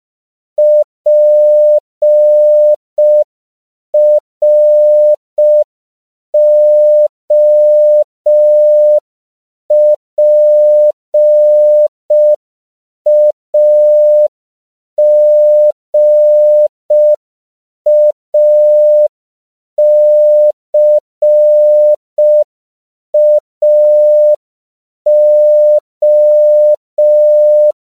Morse code, heard as a keyed tone, about 5 wpm, 600 Hz. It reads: PROPAGACAO
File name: CW / 5PPM